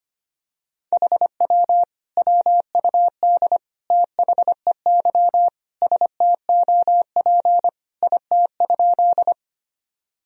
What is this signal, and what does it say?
Morse code, heard as a keyed tone, about 25 wpm, 700 Hz. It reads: HW WUD THEY STOP IT?
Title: Sentences from Top 300 Words and Abbreviations 25wpm